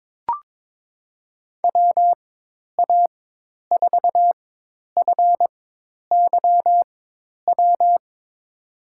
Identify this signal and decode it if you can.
Morse code, heard as a keyed tone, about 22 wpm, spaced as for 10 wpm, 700 Hz. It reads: WA4FYW
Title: USA Call Signs - 6 Characters - 4x Character Spacing - 22wpm